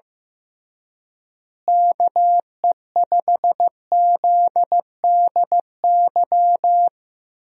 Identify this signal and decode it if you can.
Morse code, heard as a keyed tone, about 15 wpm, 700 Hz. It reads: KE5ZDY
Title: USA Call Signs - 6 Characters - Repeated 3x - 15wpm